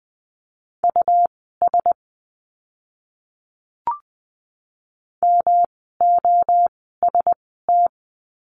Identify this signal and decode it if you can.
Morse code, heard as a keyed tone, about 20 wpm, spaced as for 15 wpm, 700 Hz. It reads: US MOST
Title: 139 Lesson 23f - Letter f - Words 20wpm